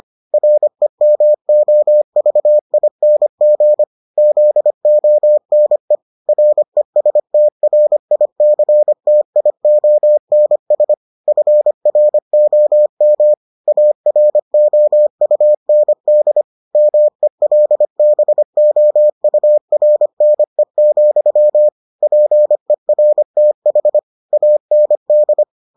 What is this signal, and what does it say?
Morse code, heard as a keyed tone, about 25 wpm, 600 Hz. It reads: REMOVING ZONE RESTRICTIONS FROM AROUND MELBOURNE, PERTH AND